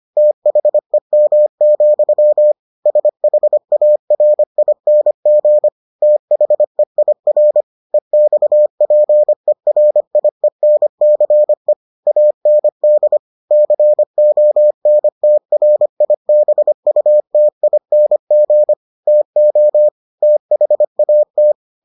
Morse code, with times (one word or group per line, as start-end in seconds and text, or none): 0.2-2.5 THEM,
2.9-5.7 SHARING
6.0-7.6 THEIR
7.9-11.7 EXPERIENCE
12.1-13.2 AND
13.5-18.7 CONTRIBUTING
19.1-19.9 TO
20.2-21.5 THAT